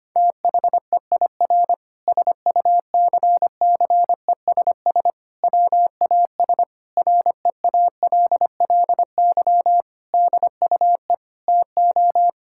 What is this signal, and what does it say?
Morse code, heard as a keyed tone, about 25 wpm, 700 Hz. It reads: THEIR SUCCESS WAS REALLY DUE TO
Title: Sentences from Top 1000 Words - Review Part 6 - 25wpm